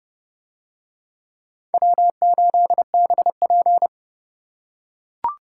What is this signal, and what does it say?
Morse code, heard as a keyed tone, about 28 wpm, 700 Hz. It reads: W8BP